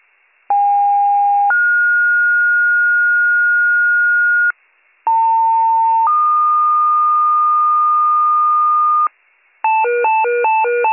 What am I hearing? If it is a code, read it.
Two-tone page, heard as a one-second tone, then a three-second tone, about 810 then 1430 Hz, about 900 then 1220 Hz, with noise. It is heard from 0.5 to 9.1 s.